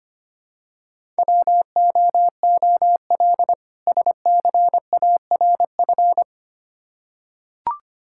Morse code, heard as a keyed tone, about 25 wpm, 700 Hz. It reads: WOOL SCARF
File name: practice / 25wpm / Difficult Words with Hints - 25wpm